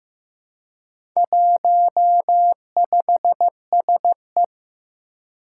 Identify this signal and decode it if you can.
Morse code, heard as a keyed tone, about 15 wpm, 700 Hz. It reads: 15SE
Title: Four Challenging Letter-Number Combinations 15wpm